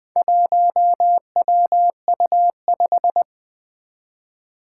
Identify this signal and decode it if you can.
Morse code, heard as a keyed tone, about 20 wpm, 700 Hz. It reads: 1WU5
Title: Four Challenging Letter-Number Combinations 20wpm